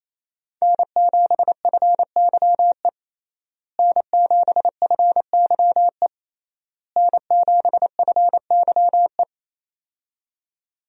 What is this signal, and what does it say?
Morse code, heard as a keyed tone, about 27 wpm, 700 Hz. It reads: N7FYE N7FYE N7FYE
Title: USA Call Signs - 5 Characters - Repeated 3x - 28wpm